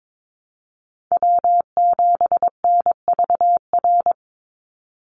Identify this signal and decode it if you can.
Morse code, heard as a keyed tone, about 22 wpm, 700 Hz. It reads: W7NVR
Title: USA Call Signs 22wpm